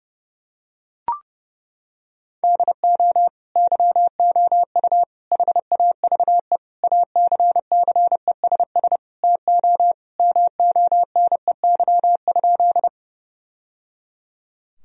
Morse code, heard as a keyed tone, about 30 wpm, 700 Hz. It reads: DO YOU HAVE ACCESS TO MONEY?
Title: Sentences from Top 1000 Words - Review Part 4 - 30wpm